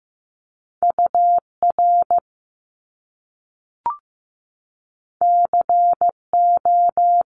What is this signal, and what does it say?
Morse code, heard as a keyed tone, about 15 wpm, 700 Hz. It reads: UR CO
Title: Top Two Letter Combinations in Words 15wpm